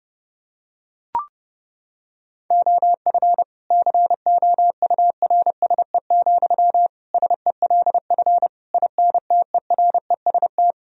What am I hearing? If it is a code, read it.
Morse code, heard as a keyed tone, about 32 wpm, 700 Hz. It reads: OF COURSE, SELF INTEREST
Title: Sentences from Top 1000 Words - Review Part 2 - 30wpm